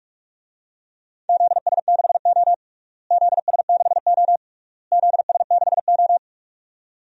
Morse code, heard as a keyed tone, about 46 wpm, 700 Hz. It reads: ZS6O ZS6O ZS6O
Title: DXCC-100 Call Signs - 4 Characters - Repeated 3x - 45wpm